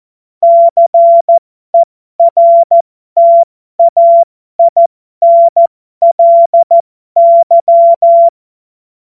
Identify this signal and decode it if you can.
Morse code, heard as a keyed tone, about 14 wpm, spaced as for 12 wpm, 680 Hz. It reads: CERTAINLY
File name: Morsekurs / Witze